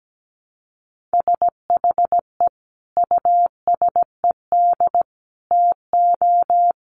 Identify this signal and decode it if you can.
Morse code, heard as a keyed tone, about 17 wpm, 700 Hz. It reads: SHE USED TO
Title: Sentences from Top 1300 Words 17wpm